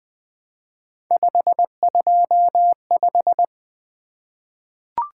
Morse code, heard as a keyed tone, about 20 wpm, 700 Hz. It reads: 525
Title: Three Letter-Number Combinations - 20wpm